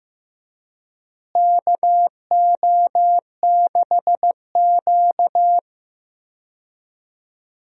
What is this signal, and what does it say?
Morse code, heard as a keyed tone, about 15 wpm, 700 Hz. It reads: KO6Q
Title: USA Call Signs - 4 Characters - Repeated 2x - 15wpm